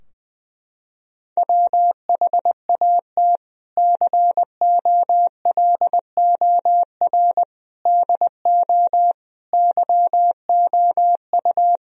Morse code, heard as a keyed tone, about 20 wpm, 700 Hz. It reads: WHAT COLOR DO YOU
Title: Sentences from Top 1000 Words - Review Part 2 - 20wpm